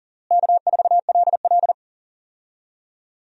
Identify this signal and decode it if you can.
Morse code, heard as a keyed tone, about 40 wpm, 700 Hz. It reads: K4LL
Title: USA Call Signs - 4 Characters - Repeated 3x - 40wpm